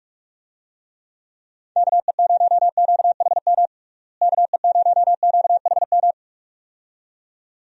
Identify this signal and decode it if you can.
Morse code, heard as a keyed tone, about 48 wpm, 700 Hz. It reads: KE0QHM KE0QHM
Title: USA Call Signs - 6 Characters - Repeated 2x - 45wpm